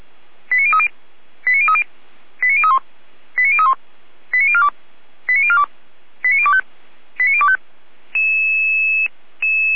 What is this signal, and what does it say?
Five-tone selective call (ZVEI), heard as quick runs of five tones, tones of about 70 ms. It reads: 89030 89030 89031 89031 89042 89042 89025 89025
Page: /455/SDR/ZVEI/